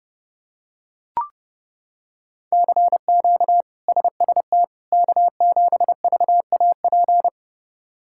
Morse code, heard as a keyed tone, about 31 wpm, 700 Hz. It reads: CQ SST K7VAP
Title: K1USN SST QSO Practice - 30wpm